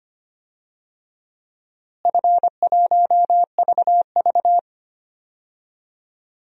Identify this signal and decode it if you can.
Morse code, heard as a keyed tone, about 25 wpm, 700 Hz. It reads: F1VV
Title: DXCC-100 Call Signs - 4 Characters - Repeated 3x - 25wpm